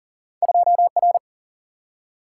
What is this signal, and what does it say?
Morse code, heard as a keyed tone, about 42 wpm, 700 Hz. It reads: 2R
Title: Single-Letter Single-Number 40wpm